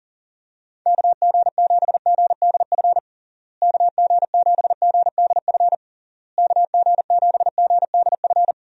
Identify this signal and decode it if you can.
Morse code, heard as a keyed tone, about 40 wpm, 700 Hz. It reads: KG7GDF KG7GDF KG7GDF
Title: USA Call Signs - 6 Characters - Repeated 3x - 40wpm